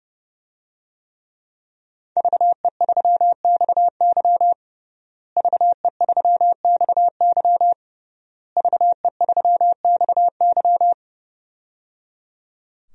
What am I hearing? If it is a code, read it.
Morse code, heard as a keyed tone, about 30 wpm, 700 Hz. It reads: VE3XY VE3XY VE3XY
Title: DXCC-100 Call Signs - 5 Characters - Repeated 3x - 30wpm